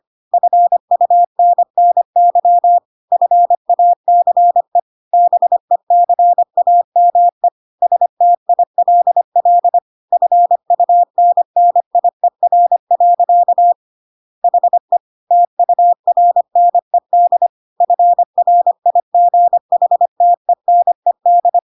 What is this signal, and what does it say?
Morse code, heard as a keyed tone, about 25 wpm, 700 Hz. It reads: FUNNY FACE BECAME STILL FUNNIER. HE TURNED FRIGHTENED
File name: Pinocchio / Pinocchio ch1 25 25 ik1hge